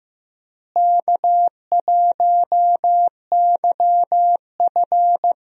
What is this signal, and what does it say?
Morse code, heard as a keyed tone, about 16 wpm, 700 Hz. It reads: K1YF
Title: USA Call Signs - 4 Characters - Repeated 3x - 15wpm